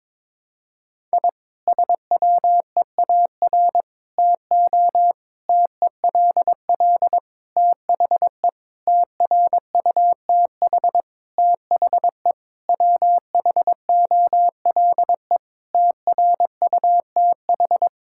Morse code, heard as a keyed tone, about 22 wpm, 700 Hz. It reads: I SWEAR TO TELL THE TRUTH THE WHOLE TRUTH
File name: Sentences with 9-Word Phrases - 22wpm